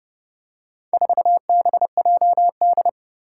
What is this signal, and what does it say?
Morse code, heard as a keyed tone, about 28 wpm, 700 Hz. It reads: 4BJD
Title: Four Challenging Letter-Number Combinations 30wpm